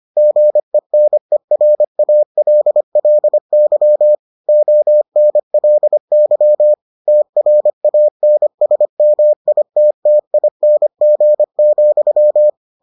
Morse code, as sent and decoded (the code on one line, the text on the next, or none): --. -. .-. .- .-.. .-.. -.-- / --- -. .-.. -.-- / - .-. .- -. ... -- .. - - .. -. --. --..--
GENERALLY ONLY TRANSMITTING,